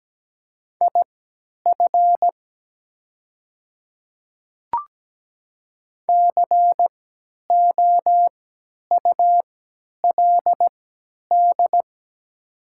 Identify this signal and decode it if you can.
Morse code, heard as a keyed tone, about 17 wpm, spaced as for 10 wpm, 700 Hz. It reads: IF COULD
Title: 3x Character Spacing - Top 300 Words - 17wpm